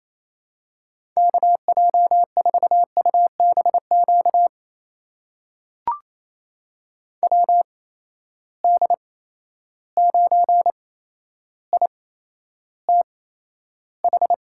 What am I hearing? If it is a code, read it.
Morse code, heard as a keyed tone, about 27 wpm, spaced as for 12 wpm, 700 Hz. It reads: KJ4UBQ WD9ITH
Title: USA Call Signs - 6 Characters - 8x Character Spacing - 28wpm